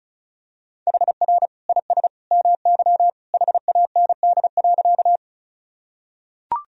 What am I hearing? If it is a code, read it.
Morse code, heard as a keyed tone, about 36 wpm, 690 Hz. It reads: HR IS MY HAND.
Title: Sentences from Top 200 Words and Abbreviations 35wpm